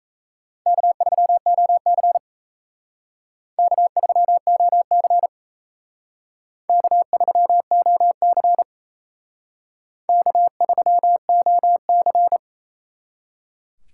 Morse code, heard as a keyed tone, about 35 wpm, 700 Hz. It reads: K3OC K3OC K3OC K3OC
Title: Speed Racing - USA 1x2 Call Signs 28wpm